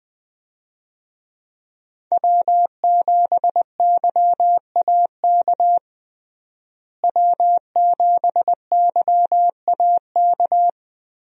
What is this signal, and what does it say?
Morse code, heard as a keyed tone, about 21 wpm, 700 Hz. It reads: W7YAK W7YAK
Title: USA Call Signs - 5 Characters - Repeated 3x - 20wpm